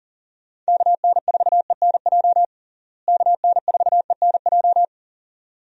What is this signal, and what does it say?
Morse code, heard as a keyed tone, about 40 wpm, 700 Hz. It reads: KN4ENJ KN4ENJ
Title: USA Call Signs - 6 Characters - Repeated 2x - 40wpm